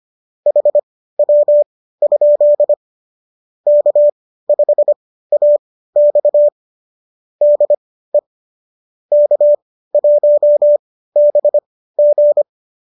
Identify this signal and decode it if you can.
Morse code, heard as a keyed tone, about 25 wpm, spaced as for 15 wpm, 590 Hz. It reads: HW? K5AX DE K1BG
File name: qso201 15